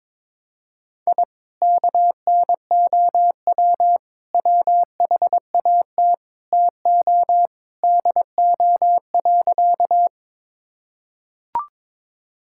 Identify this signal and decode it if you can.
Morse code, heard as a keyed tone, about 21 wpm, 700 Hz. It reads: I KNOW WHAT TO DO.
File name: practice / 22wpm / Sentences with 4-Word Phrases - 22wpm